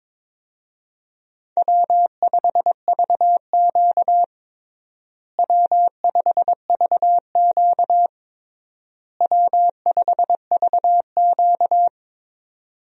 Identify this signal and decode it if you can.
Morse code, heard as a keyed tone, about 22 wpm, 700 Hz. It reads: W5VQ W5VQ W5VQ
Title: USA Call Signs - 4 Characters - Repeated 3x - 22wpm